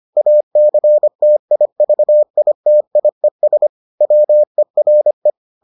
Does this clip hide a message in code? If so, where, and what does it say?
Yes: Morse code from 0.2 to 5.3 s, heard as a keyed tone, about 25 wpm, 600 Hz. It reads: ACTIVITIES WERE